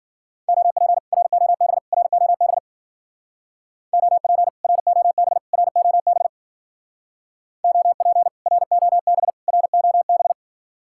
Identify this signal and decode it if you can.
Morse code, heard as a keyed tone, about 51 wpm, 700 Hz. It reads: OP ROB ROB OP ROB ROB OP ROB ROB